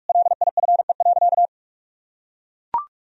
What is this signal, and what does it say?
Morse code, heard as a keyed tone, about 41 wpm, 700 Hz. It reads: LIFE.
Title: Sentences from Top 1000 Words - Review Part 1 - 45wpm